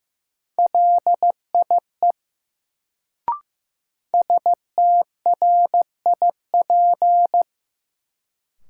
Morse code, heard as a keyed tone, about 15 wpm, 700 Hz. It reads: LIE STRIP
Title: Sentences from Top 1600 Words - New Word Drill - 15wpm